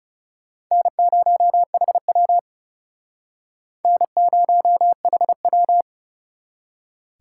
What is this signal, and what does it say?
Morse code, heard as a keyed tone, about 36 wpm, 700 Hz. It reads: N0HW N0HW
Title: Speed Racing - USA 1x2 Call Signs 30wpm